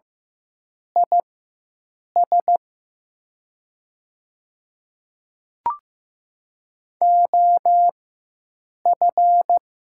Morse code, heard as a keyed tone, about 15 wpm, spaced as for 7 wpm, 700 Hz. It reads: IS OF
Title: 4x Character Spacing - Top 200 Words - 15wpm